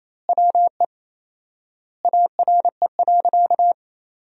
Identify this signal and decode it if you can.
Morse code, heard as a keyed tone, about 28 wpm, 700 Hz. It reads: WE ARE.
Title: 4x Word-Spacing - Sentences from Top 200 Words - 28wpm